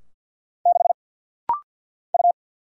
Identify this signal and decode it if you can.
Morse code, heard as a keyed tone, about 49 wpm, 700 Hz. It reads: 6 U
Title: Single Letter-Number Alt - Rapid-Fire 50wpm